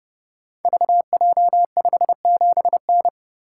Morse code, heard as a keyed tone, about 30 wpm, 700 Hz. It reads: VJ57N
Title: Code Groups - 5 Letter:Numbers - 30wpm